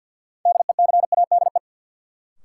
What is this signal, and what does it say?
Morse code, heard as a keyed tone, about 53 wpm, 700 Hz. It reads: DECADE
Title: Sets of 5 Words - New Word Drill - 50wpm